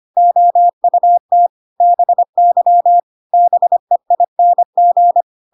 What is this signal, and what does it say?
Morse code, heard as a keyed tone, about 25 wpm, 700 Hz. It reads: OUT BY BEING
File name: Pinocchio / Pinocchio ch3 25 25 ik1hge